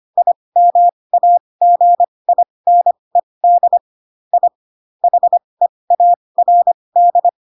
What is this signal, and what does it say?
Morse code, heard as a keyed tone, about 25 wpm, spaced as for 20 wpm, 700 Hz. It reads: IMAGINED I HEARD
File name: Pinocchio / Pinocchio ch1 15 25 ik1hge